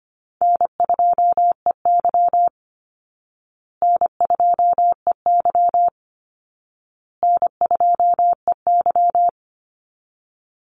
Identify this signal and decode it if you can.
Morse code, heard as a keyed tone, about 25 wpm, 700 Hz. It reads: N2EY N2EY N2EY
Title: Easy Call Signs Repeated 3x - 25wpm